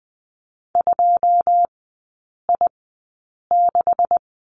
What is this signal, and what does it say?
Morse code, heard as a keyed tone, about 20 wpm, spaced as for 8 wpm, 700 Hz. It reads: 2I6
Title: 145 Lesson 24e - Letter y - Three Characters 20wpm